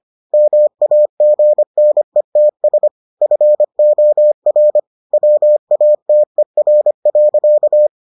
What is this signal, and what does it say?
Morse code, heard as a keyed tone, about 25 wpm, 600 Hz. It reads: MAGNETS FOR WATER.